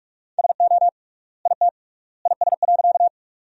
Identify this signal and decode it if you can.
Morse code, heard as a keyed tone, about 43 wpm, 700 Hz. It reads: SO IT IS.
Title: 3x Word-Spacing - Sentences from Top 100 Words - 45wpm